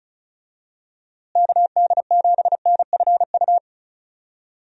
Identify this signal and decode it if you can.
Morse code, heard as a keyed tone, about 36 wpm, 700 Hz. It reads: KD7NFU
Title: USA Call Signs - 6 Characters - Repeated 2x - 35wpm